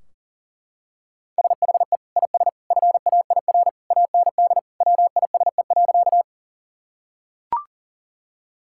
Morse code, heard as a keyed tone, about 40 wpm, 700 Hz. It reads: SHE IS FAIR AND WISE.
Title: Sentences from Top 1600 Words 40wpm